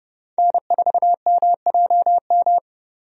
Morse code, heard as a keyed tone, about 30 wpm, 700 Hz. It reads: N4MJM
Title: USA Call Signs - 5 Characters - 30wpm